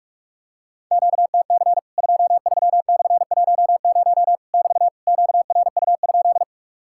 Morse code, heard as a keyed tone, about 44 wpm, 700 Hz. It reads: QTC 23/10 <BT> QRU?
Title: Worked All Europe DX Contest - QTC Exchange - 45wpm